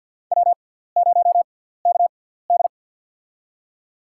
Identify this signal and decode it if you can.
Morse code, heard as a keyed tone, about 52 wpm, spaced as for 17 wpm, 700 Hz. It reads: W0KD
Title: USA Call Signs - 6x Character Spacing - 50wpm